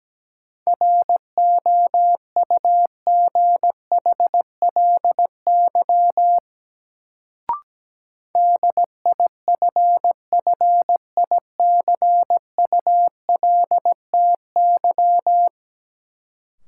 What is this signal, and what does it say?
Morse code, heard as a keyed tone, about 17 wpm, 700 Hz. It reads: ROUGHLY DIFFICULTY
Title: Sentences from Top 1500 Words - New Word Drill - 17wpm